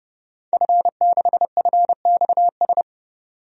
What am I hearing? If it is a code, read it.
Morse code, heard as a keyed tone, about 30 wpm, 700 Hz. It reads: F6FXS
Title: DXCC-100 Call Signs - 5 Characters - 30wpm